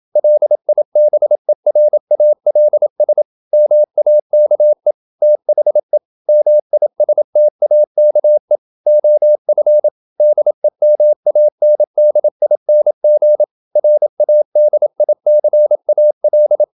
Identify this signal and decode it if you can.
Morse code, heard as a keyed tone, about 27 wpm, 600 Hz. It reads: LIBERALS MAKE THE MISTAKE OF DEMANDING RADICAL
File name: Never 0027wpm